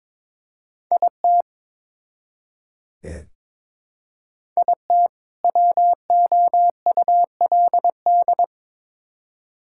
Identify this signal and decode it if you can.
Morse code, heard as a keyed tone, about 22 wpm, 700 Hz. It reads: IT IT WOULD
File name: Build A Phrase - 2-Word Phrases - 22wpm